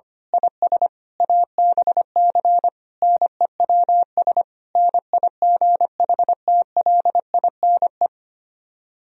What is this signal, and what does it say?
Morse code, heard as a keyed tone, about 24 wpm, 700 Hz. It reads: IS ABC NEWS NIGHTLINE
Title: Speed-Racing - Sets of 5 Words 25wpm